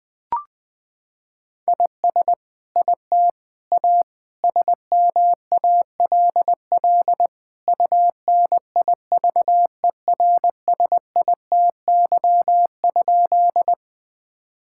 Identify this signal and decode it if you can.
Morse code, heard as a keyed tone, about 20 wpm, 700 Hz. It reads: IS IT A SMALL UNIVERSITY?
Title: Sentences from Top 1000 Words - Review Part 5 - 20wpm